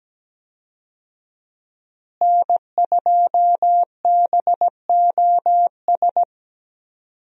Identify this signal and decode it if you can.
Morse code, heard as a keyed tone, about 17 wpm, 700 Hz. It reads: N2BOS
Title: USA Call Signs - 5 Characters - Repeated 3x - 17wpm